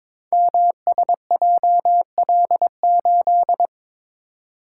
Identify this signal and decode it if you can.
Morse code, heard as a keyed tone, about 22 wpm, 700 Hz. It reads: MSJL8
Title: Code Groups - 5 Letter:Numbers - 22wpm